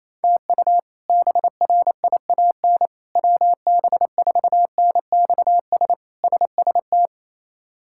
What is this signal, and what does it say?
Morse code, heard as a keyed tone, about 28 wpm, 700 Hz. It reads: TU BRIAN WB4NXS SST
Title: K1USN SST QSO Practice - 28wpm